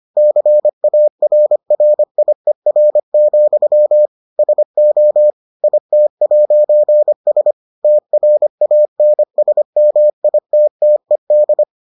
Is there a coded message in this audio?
Yes: Morse code from 0.2 to 11.6 s, heard as a keyed tone, about 25 wpm, 600 Hz. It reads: CARRIER, SO IT'S TRANSMITTED